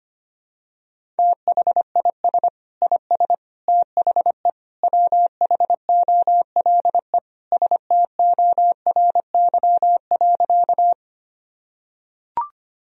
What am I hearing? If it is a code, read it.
Morse code, heard as a keyed tone, about 25 wpm, 700 Hz. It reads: THIS IS THE WHOLE STORY.
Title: Sentences from Top 1000 Words - Review Part 1 - 25wpm